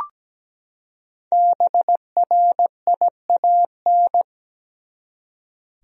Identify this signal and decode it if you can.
Morse code, heard as a keyed tone, about 17 wpm, 700 Hz. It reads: BRIAN
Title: USA Common Names - Max 5 Letters - 17wpm